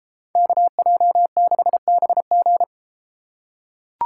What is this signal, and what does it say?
Morse code, heard as a keyed tone, about 30 wpm, 700 Hz. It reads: KJ6BG